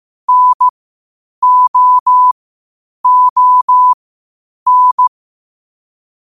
Morse code, heard as a keyed tone, about 15 wpm, spaced as for 8 wpm, 1000 Hz. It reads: NOON
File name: qotdfiles / 8wpm